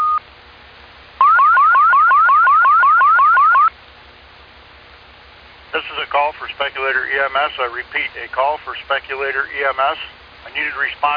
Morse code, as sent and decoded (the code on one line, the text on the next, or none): none